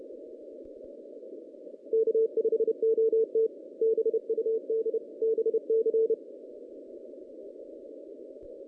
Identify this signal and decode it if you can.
Morse code, heard as a keyed tone, about 28 wpm, 440 Hz. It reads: K5OT BUD BC
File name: 2016 JAN CW NAQP / 2016 Jan naqp k5ot - 0-6